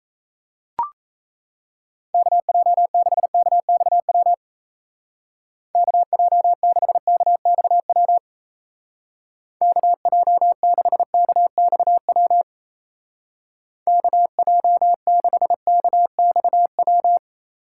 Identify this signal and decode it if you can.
Morse code, heard as a keyed tone, about 35 wpm, 700 Hz. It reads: KJ6KXW KJ6KXW KJ6KXW KJ6KXW
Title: Speed Racing - USA 2x3 Call Signs 28wpm